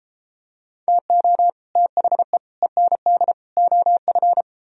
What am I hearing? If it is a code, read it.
Morse code, heard as a keyed tone, about 33 wpm, 700 Hz. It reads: TO THE END OF